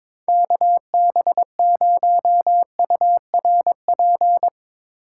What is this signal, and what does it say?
Morse code, heard as a keyed tone, about 22 wpm, 700 Hz. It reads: KB0URP
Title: USA Call Signs - 2x Character Spacing - 22wpm